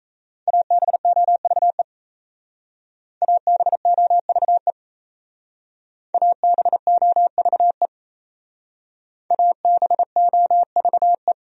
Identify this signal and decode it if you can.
Morse code, heard as a keyed tone, about 37 wpm, 700 Hz. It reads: ABOVE ABOVE ABOVE ABOVE